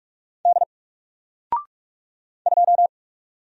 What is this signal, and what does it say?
Morse code, heard as a keyed tone, about 44 wpm, 700 Hz. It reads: D 2
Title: Single Letter-Number Alt 45wpm